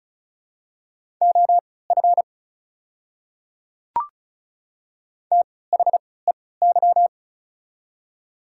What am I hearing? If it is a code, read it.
Morse code, heard as a keyed tone, about 36 wpm, spaced as for 20 wpm, 690 Hz. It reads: OF THEY
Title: 3x Character Spacing - Top 100 Words - 35wpm